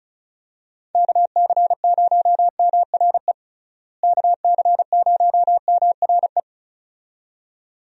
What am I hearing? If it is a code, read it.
Morse code, heard as a keyed tone, about 34 wpm, 700 Hz. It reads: KC0MRE KC0MRE
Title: USA Call Signs - 6 Characters - Repeated 2x - 35wpm